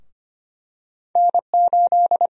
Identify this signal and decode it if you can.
Morse code, heard as a keyed tone, about 25 wpm, 700 Hz. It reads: N8
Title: USA Call Sign Prefixes - 25wpm